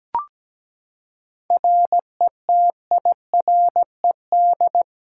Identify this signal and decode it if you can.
Morse code, heard as a keyed tone, about 17 wpm, 700 Hz. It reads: RETIRED
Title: Practice QSO - Exchange 3 - More Info - 17wpm